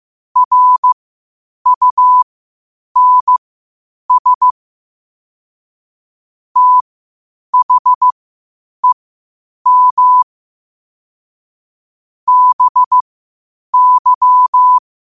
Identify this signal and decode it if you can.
Morse code, heard as a keyed tone, about 15 wpm, spaced as for 8 wpm, 1000 Hz. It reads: RUNS THEM BY
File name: qotdfiles / 8wpm